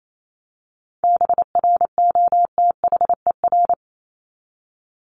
Morse code, heard as a keyed tone, about 28 wpm, 700 Hz. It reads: BROTHER
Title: Top 1751-2000 Words 28wpm